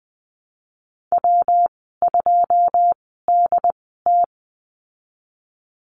Morse code, heard as a keyed tone, about 20 wpm, spaced as for 15 wpm, 700 Hz. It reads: W2DT
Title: 193 Lesson 31g - Letter v - Call Signs 20wpm